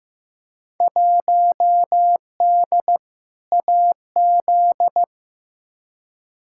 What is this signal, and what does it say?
Morse code, heard as a keyed tone, about 15 wpm, 700 Hz. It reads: 1D AZ
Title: Field Day Exchanges - 15wpm